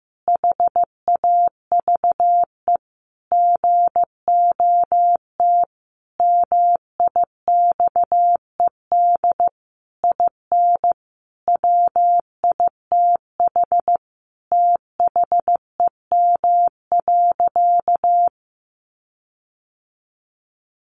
Morse code, heard as a keyed tone, about 14 wpm, 700 Hz. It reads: HAVE GOT MIXED IN WITH THEM.
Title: Sentences from Top 1500 Words 15wpm